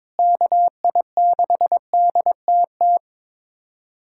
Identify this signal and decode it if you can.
Morse code, heard as a keyed tone, about 22 wpm, 700 Hz. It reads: KI6DTT
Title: USA Callsign DE Callsign - 6 Characters - 22wpm